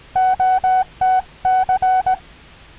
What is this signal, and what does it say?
Morse code, heard as a keyed tone, about 18 wpm, 750 Hz. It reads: OTC
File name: 6M audio / 9m2to cw cycle23